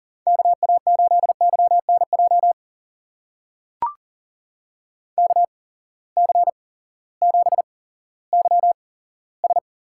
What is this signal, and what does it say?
Morse code, heard as a keyed tone, about 40 wpm, spaced as for 18 wpm, 700 Hz. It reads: KA8YNJ KC7YS
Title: USA Call Signs - 6 Characters - 8x Character Spacing - 40wpm